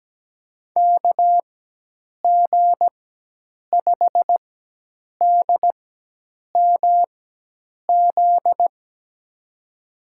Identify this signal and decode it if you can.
Morse code, heard as a keyed tone, about 17 wpm, spaced as for 8 wpm, 700 Hz. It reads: KG5DMZ
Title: USA Call Signs - 6 Characters - 4x Character Spacing - 17wpm